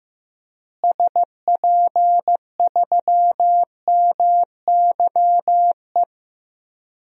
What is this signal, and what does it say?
Morse code, heard as a keyed tone, about 15 wpm, 700 Hz. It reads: SP3MYE
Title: DXCC-100 Call Signs - 6 Characters - Repeated 3x - 15wpm